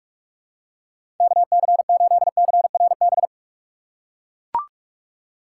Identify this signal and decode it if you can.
Morse code, heard as a keyed tone, about 43 wpm, 700 Hz. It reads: KC8CRB
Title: USA Call Signs - 6 Characters - 3x Character Spacing - 45wpm